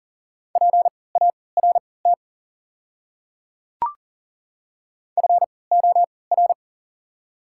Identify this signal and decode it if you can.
Morse code, heard as a keyed tone, about 38 wpm, spaced as for 22 wpm, 700 Hz. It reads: PART FOR
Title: 3x Character Spacing - Top 500 Words - 40wpm